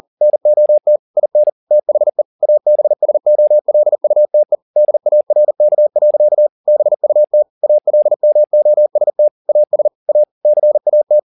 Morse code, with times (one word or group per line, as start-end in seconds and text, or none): 0.2-1.0 NOT
1.2-1.5 IN
1.7-2.2 THE
2.4-4.6 ABSOLUTE
4.8-6.5 DARK.
6.7-7.4 BUT
7.6-9.3 ALMOST
9.5-9.9 AS
10.1-10.2 A
10.5-11.2 CAT